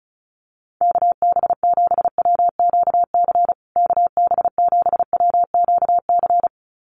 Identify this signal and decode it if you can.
Morse code, heard as a keyed tone, about 34 wpm, 700 Hz. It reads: KB7WQC KB7WQC
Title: USA Call Signs Repeated 2x 35wpm